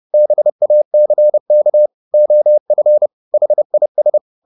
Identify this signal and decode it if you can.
Morse code, heard as a keyed tone, about 30 wpm, 600 Hz. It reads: BACK OF HIS
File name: the old man and the sea / the old man and the sea 1 30wpm